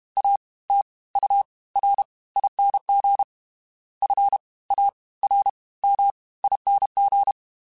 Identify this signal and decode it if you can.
Morse code, heard as a keyed tone, about 32 wpm, 800 Hz. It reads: A T U R ING F A R M ING
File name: Special INGs / Special INGs-mp3s / Fast ing @30